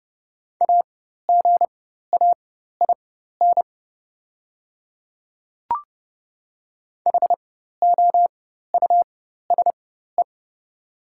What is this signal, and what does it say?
Morse code, heard as a keyed tone, about 32 wpm, spaced as for 14 wpm, 690 Hz. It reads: AGAIN HOUSE
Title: 4x Character Spacing - Top 200 Words - 30wpm